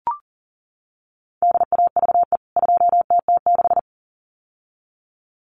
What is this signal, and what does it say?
Morse code, heard as a keyed tone, about 40 wpm, 700 Hz. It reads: DAVE 2TT6
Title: CWT Exchange 40wpm